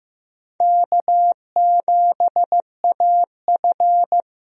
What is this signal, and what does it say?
Morse code, heard as a keyed tone, about 15 wpm, 700 Hz. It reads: K7AF
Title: USA Call Signs - 4 Characters - Repeated 3x - 15wpm